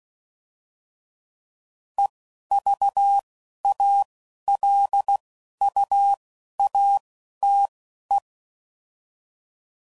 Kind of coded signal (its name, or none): Morse code